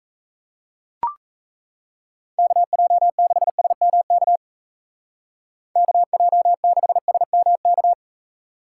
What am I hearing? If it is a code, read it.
Morse code, heard as a keyed tone, about 34 wpm, 700 Hz. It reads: KJ6SMK KJ6SMK
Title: Speed Racing - USA 2x3 Call Signs 28wpm